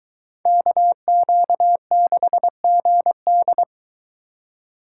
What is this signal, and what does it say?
Morse code, heard as a keyed tone, about 22 wpm, 700 Hz. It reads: KQ6GD